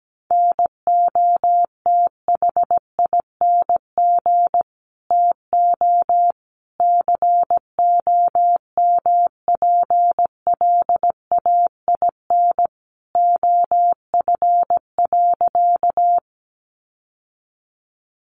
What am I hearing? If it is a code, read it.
Morse code, heard as a keyed tone, about 17 wpm, 700 Hz. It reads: NOTHING TO COMPLAIN OF.
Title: Sentences from Top 1300 Words 17wpm